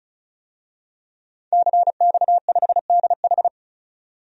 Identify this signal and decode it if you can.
Morse code, heard as a keyed tone, about 36 wpm, 700 Hz. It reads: CX5DH
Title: DXCC-100 Call Signs - 5 Characters - 35wpm